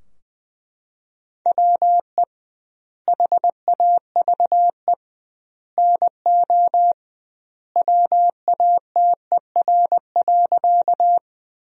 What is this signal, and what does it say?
Morse code, heard as a keyed tone, about 20 wpm, 700 Hz. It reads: WE HAVE NO WATER.
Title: 2x Word-Spacing - Sentences from Top 100 Words - 20wpm